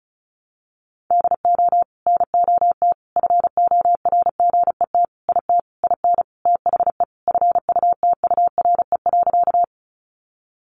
Morse code, heard as a keyed tone, about 34 wpm, 700 Hz. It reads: DO NOT FORGET IT IN THE FUTURE.
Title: Sentences from Top 1200 Words 35wpm